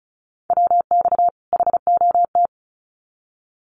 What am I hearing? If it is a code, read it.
Morse code, heard as a keyed tone, about 34 wpm, 700 Hz. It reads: WX HOT
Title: Common QSO Phrases 35wpm